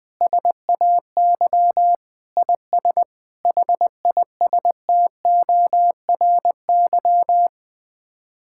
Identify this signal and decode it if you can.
Morse code, heard as a keyed tone, about 20 wpm, 700 Hz. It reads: SAY IS HISTORY
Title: Sets of 8 Words - 20wpm